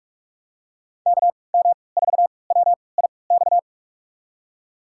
Morse code, heard as a keyed tone, about 41 wpm, spaced as for 25 wpm, 700 Hz. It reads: KM4WIX
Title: USA Call Signs - 6 Characters - 3x Character Spacing - 45wpm